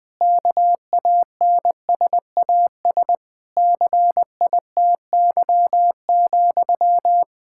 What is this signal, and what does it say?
Morse code, heard as a keyed tone, about 20 wpm, 700 Hz. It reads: KANSAS CITY,
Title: K1USN SST QSO Practice 20wpm